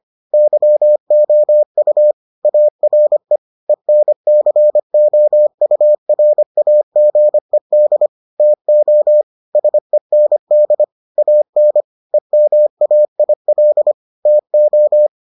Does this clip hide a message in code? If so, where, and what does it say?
Yes: Morse code from 0.3 to 15.1 s, heard as a keyed tone, about 25 wpm, 600 Hz. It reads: YOU ARE ENCOURAGED TO SEND AN EMAIL TO